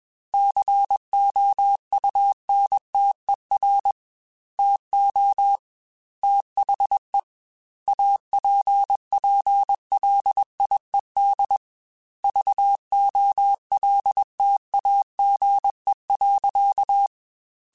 Morse code, heard as a keyed tone, about 21 wpm, 800 Hz. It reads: COUNTER TO THE APPLIED VOLTAGE.
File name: Technical Subjects / Technical Subjects-mp3s / XL @20